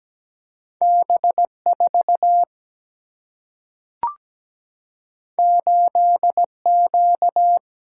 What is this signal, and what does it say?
Morse code, heard as a keyed tone, about 17 wpm, 700 Hz. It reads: B 4 8 Q
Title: Single-Letter Single-Number 17wpm